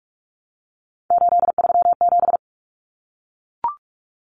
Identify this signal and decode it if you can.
Morse code, heard as a keyed tone, about 44 wpm, 700 Hz. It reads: 837
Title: Numbers - 3-Digit 45wpm